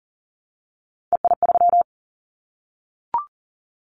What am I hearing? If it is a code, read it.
Morse code, heard as a keyed tone, about 39 wpm, 700 Hz. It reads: EI3
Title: Callsign Prefixes for Top 50 DXCC - 40wpm